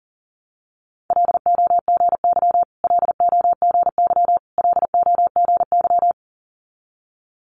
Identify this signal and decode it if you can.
Morse code, heard as a keyed tone, about 41 wpm, 700 Hz. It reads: LOGY LOGY LOGY
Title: Multi-letter Phonemes Repeated 3x 40wpm